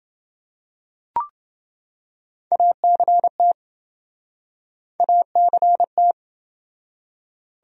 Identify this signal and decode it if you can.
Morse code, heard as a keyed tone, about 28 wpm, 700 Hz. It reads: ACT ACT